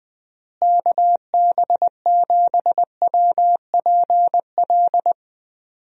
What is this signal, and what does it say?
Morse code, heard as a keyed tone, about 20 wpm, 700 Hz. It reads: KB7WPL
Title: USA Call Signs - 6 Characters - Repeated 2x - 20wpm